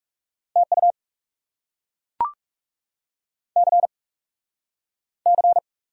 Morse code, heard as a keyed tone, about 40 wpm, 700 Hz. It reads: TU C C